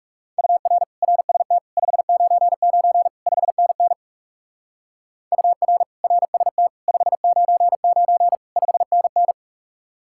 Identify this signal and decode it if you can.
Morse code, heard as a keyed tone, about 36 wpm, 700 Hz. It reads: UR RST 599 5NN UR RST 599 5NN
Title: Speed-Racing Common QSO Phrases - 30wpm